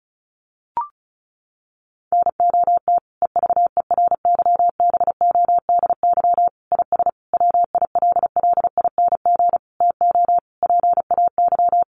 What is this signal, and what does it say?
Morse code, heard as a keyed tone, about 34 wpm, 700 Hz. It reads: NOT EVERYBODY IS WILLING TO PAY
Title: Sentences from Top 1400 Words 35wpm